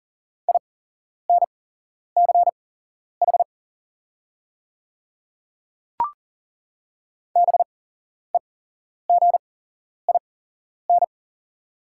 Morse code, heard as a keyed tone, about 40 wpm, spaced as for 11 wpm, 700 Hz. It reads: INCH BEGIN
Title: 8x Character Spacing - Top 500 Words - 40wpm